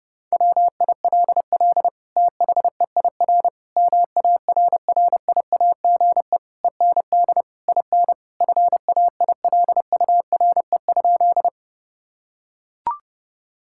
Morse code, heard as a keyed tone, about 30 wpm, 700 Hz. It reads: WILL THEIR MARRIAGE END IN FAILURE?
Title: Sentences from Top 1000 Words - Review Part 6 - 30wpm